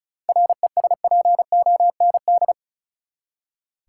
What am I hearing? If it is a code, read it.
Morse code, heard as a keyed tone, about 34 wpm, 700 Hz. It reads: RESPOND
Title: Seven Letter Words - 35wpm